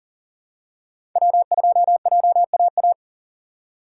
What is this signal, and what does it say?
Morse code, heard as a keyed tone, about 40 wpm, 700 Hz. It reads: W2JAA
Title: USA Call Signs - 5 Characters - 4x Character Spacing - 40wpm